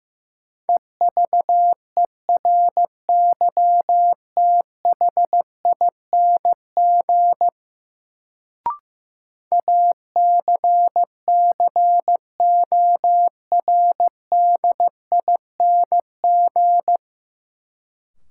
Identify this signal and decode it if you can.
Morse code, heard as a keyed tone, about 15 wpm, 700 Hz. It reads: EVERYTHING ACCORDING
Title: Sentences from Top 700 Words - New Word Drill - 15wpm